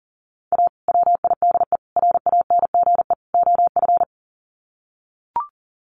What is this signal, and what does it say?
Morse code, heard as a keyed tone, about 40 wpm, 700 Hz. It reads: A WIDE RANGE OF
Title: Sets of 4 Words 40wpm